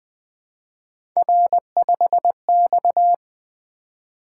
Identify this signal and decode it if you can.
Morse code, heard as a keyed tone, about 20 wpm, 700 Hz. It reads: R5X
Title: Three Letter-Number Combinations - 20wpm